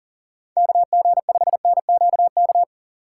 Morse code, heard as a keyed tone, about 40 wpm, 700 Hz. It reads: KG5NQK
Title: USA Call Signs - 6 Characters - 3x Character Spacing - 40wpm